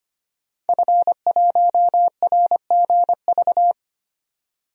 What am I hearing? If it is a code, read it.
Morse code, heard as a keyed tone, about 25 wpm, 700 Hz. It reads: F1RGV
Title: DXCC-100 Call Signs - 5 Characters - 25wpm